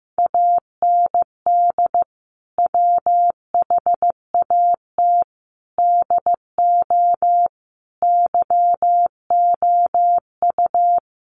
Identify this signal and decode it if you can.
Morse code, heard as a keyed tone, about 15 wpm, 700 Hz. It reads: AND WHAT DO YOU
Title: Sentences from Top 100 Words 15wpm